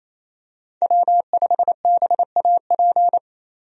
Morse code, heard as a keyed tone, about 27 wpm, 700 Hz. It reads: W5BAP
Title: USA Call Signs - 5 Characters - 28wpm